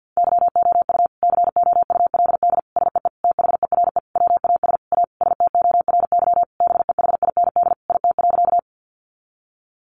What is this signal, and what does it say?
Morse code, heard as a keyed tone, about 50 wpm, 700 Hz. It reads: YOU COULD SEE THERE WAS A STORY BEHIND IT.
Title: Sentences from Top 500 Words 50wpm